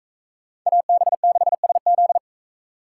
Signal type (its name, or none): Morse code